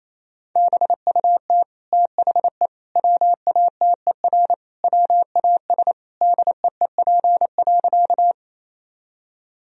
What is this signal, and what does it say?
Morse code, heard as a keyed tone, about 28 wpm, 700 Hz. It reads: BUT THE WATER WAS DEEP.
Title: Sentences from Top 1000 Words - Review Part 3 - 28wpm